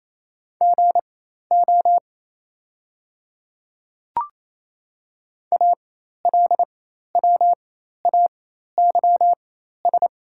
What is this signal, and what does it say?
Morse code, heard as a keyed tone, about 28 wpm, spaced as for 13 wpm, 690 Hz. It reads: GO ALWAYS
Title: 4x Character Spacing - Top 300 Words - 28wpm